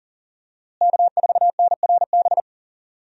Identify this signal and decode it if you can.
Morse code, heard as a keyed tone, about 40 wpm, 700 Hz. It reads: K4NRB
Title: USA Call Signs - 5 Characters - Repeated 3x - 40wpm